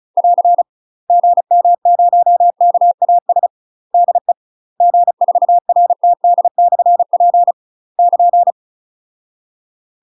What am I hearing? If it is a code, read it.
Morse code, heard as a keyed tone, about 36 wpm, 700 Hz. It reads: <AR> GM0KAS DE G4RTD/P <KN>
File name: UK MASTER QSOs TRAINING / 35WPM / UK MASTER QSOs 0007